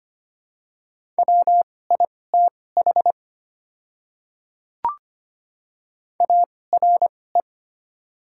Morse code, heard as a keyed tone, about 25 wpm, spaced as for 18 wpm, 700 Hz. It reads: WITH ARE
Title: 2x Character Spacing - Top 100 Words - 25wpm